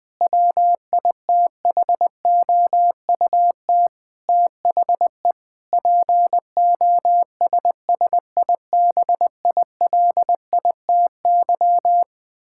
Morse code, heard as a keyed tone, about 20 wpm, 700 Hz. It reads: WITHOUT THE POSSIBILITY
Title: Sentences with 8-Word Phrases - Spoken Beginning - 20wpm